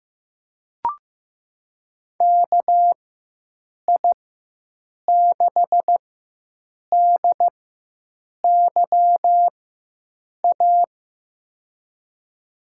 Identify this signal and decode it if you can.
Morse code, heard as a keyed tone, about 15 wpm, spaced as for 7 wpm, 700 Hz. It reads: KI6DYA
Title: USA Call Signs - 6 Characters - 4x Character Spacing - 15wpm